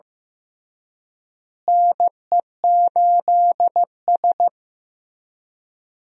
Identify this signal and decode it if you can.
Morse code, heard as a keyed tone, about 15 wpm, 700 Hz. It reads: NE8S
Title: USA Call Signs - 4 Characters - Repeated 3x - 15wpm